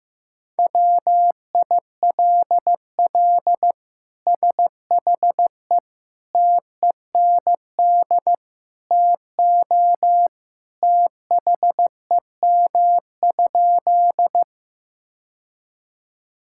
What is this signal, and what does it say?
Morse code, heard as a keyed tone, about 15 wpm, 700 Hz. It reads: WILL SHE TEND TO THEM?
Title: Sentences from Top 1000 Words - Review Part 5 - 15wpm